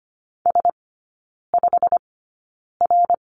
Morse code, heard as a keyed tone, about 25 wpm, spaced as for 9 wpm, 700 Hz. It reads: S5R
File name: Lesson 17e - Letter u - Three Characters 25wpm